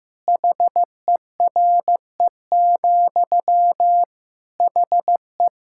Morse code, heard as a keyed tone, about 16 wpm, 700 Hz. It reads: HERE, HE
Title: Sentences from Top 1600 Words 15wpm